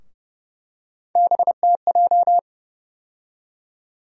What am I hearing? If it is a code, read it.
Morse code, heard as a keyed tone, about 30 wpm, 700 Hz. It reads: BTJ
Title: USA Call Sign Postfixes - 3 Letters 30wpm